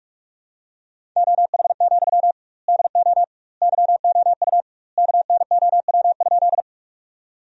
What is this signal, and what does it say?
Morse code, heard as a keyed tone, about 45 wpm, 700 Hz. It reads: OH, DO YOU KNOW?
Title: 2x Word-Spacing - Sentences from Top 500 Words - 45wpm